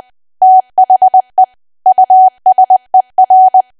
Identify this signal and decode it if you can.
Morse code, heard as a keyed tone, about 20 wpm, 750 Hz. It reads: THE USER
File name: Archive / 20 WPM / 20WPM